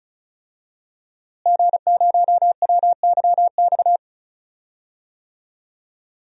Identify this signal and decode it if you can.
Morse code, heard as a keyed tone, about 36 wpm, 700 Hz. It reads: G0WYX
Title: DXCC-100 Call Signs - 5 Characters - Repeated 2x - 35wpm